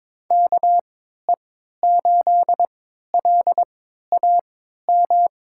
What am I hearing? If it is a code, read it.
Morse code, heard as a keyed tone, about 23 wpm, spaced as for 13 wpm, 700 Hz. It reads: KE8LAM
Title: USA Call Signs - 3x Character Spacing - 22wpm